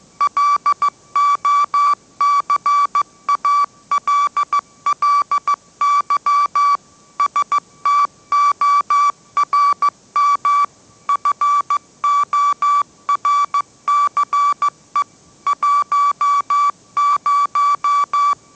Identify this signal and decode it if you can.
Morse code, heard as a keyed tone, about 15 wpm, 1150 Hz. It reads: LOCALLY STORM FORCE 10